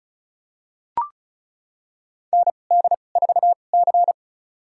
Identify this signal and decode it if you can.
Morse code, heard as a keyed tone, about 36 wpm, spaced as for 25 wpm, 700 Hz. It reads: ND4C
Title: USA Call Signs - 2x Character Spacing - 35wpm